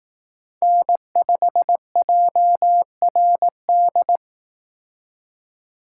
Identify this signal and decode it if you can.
Morse code, heard as a keyed tone, about 18 wpm, 700 Hz. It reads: N5JRD
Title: Speed Racing - USA 1x3 Call Signs 15wpm